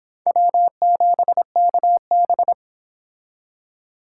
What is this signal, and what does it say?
Morse code, heard as a keyed tone, about 26 wpm, 700 Hz. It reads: W7KB